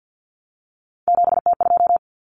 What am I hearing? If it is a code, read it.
Morse code, heard as a keyed tone, about 50 wpm, 700 Hz. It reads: 7T2
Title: Common Cut Numbers - 3-Digit 50wpm